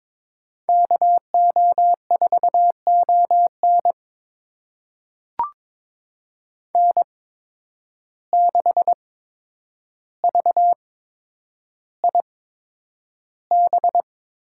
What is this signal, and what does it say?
Morse code, heard as a keyed tone, about 22 wpm, spaced as for 9 wpm, 700 Hz. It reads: KO4ON N6VIB
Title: USA Call Signs - 5 Characters - 8x Character Spacing - 22wpm